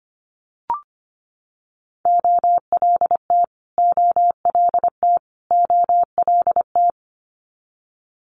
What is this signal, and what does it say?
Morse code, heard as a keyed tone, about 25 wpm, 700 Hz. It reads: OLT OLT OLT
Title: Multi-letter Phonemes Repeated 3x 25wpm